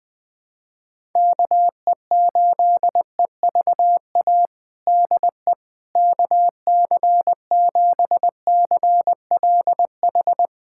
Morse code, heard as a keyed tone, about 20 wpm, 700 Hz. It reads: KE8EVA DE KC7CLH
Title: USA Callsign DE Callsign 20wpm